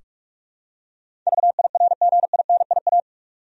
Morse code, heard as a keyed tone, about 46 wpm, 700 Hz. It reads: VIRGINIA
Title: USA Common Names - 45wpm